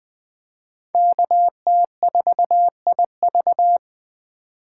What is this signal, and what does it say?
Morse code, heard as a keyed tone, about 21 wpm, 700 Hz. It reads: KT4IV
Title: Speed Racing - USA 2x2 Call Signs 17wpm